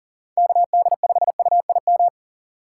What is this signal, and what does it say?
Morse code, heard as a keyed tone, about 39 wpm, 700 Hz. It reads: KD5UIM
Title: USA Call Signs - 6 Characters - 4x Character Spacing - 40wpm